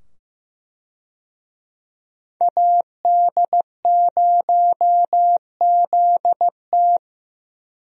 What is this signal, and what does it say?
Morse code, heard as a keyed tone, about 15 wpm, 700 Hz. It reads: AD0ZT
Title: USA Call Signs - 5 Characters - 15wpm